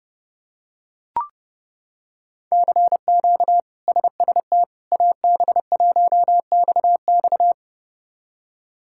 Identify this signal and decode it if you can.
Morse code, heard as a keyed tone, about 30 wpm, 700 Hz. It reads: CQ SST AB1XX
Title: K1USN SST QSO Practice - 30wpm